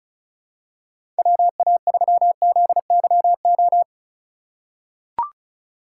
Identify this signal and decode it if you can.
Morse code, heard as a keyed tone, about 34 wpm, 700 Hz. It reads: WA3ZYO
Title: USA Call Signs - 6 Characters - 3x Character Spacing - 35wpm